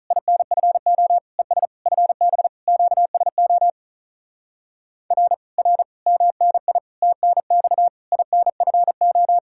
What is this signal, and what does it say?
Morse code, heard as a keyed tone, about 45 wpm, 700 Hz. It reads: INFO ES FB QSO R R MNI TNX INFO